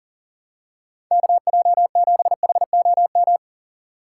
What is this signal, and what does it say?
Morse code, heard as a keyed tone, about 40 wpm, 700 Hz. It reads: KJ7HOM
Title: USA Call Signs - 4x Character Spacing - 40wpm